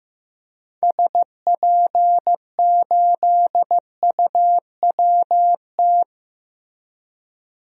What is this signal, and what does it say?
Morse code, heard as a keyed tone, about 15 wpm, 700 Hz. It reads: SP8UWT
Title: DXCC-100 Call Signs - 6 Characters - Repeated 2x - 15wpm